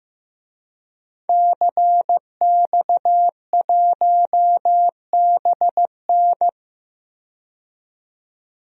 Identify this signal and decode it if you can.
Morse code, heard as a keyed tone, about 15 wpm, 700 Hz. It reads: CX1BN
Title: DXCC-100 Call Signs - 5 Characters - 15wpm